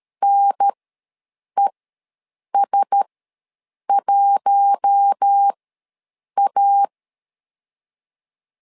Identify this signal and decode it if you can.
Morse code, heard as a keyed tone, about 13 wpm, spaced as for 7 wpm, 800 Hz. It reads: NES1A
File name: LEZIONI / 4 Q1S ws 12wpm 1827